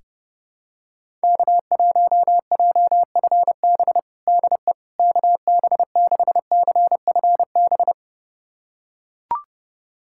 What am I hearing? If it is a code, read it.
Morse code, heard as a keyed tone, about 32 wpm, 700 Hz. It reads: K1JFB DE KB6CFB
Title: USA Callsign DE Callsign 30wpm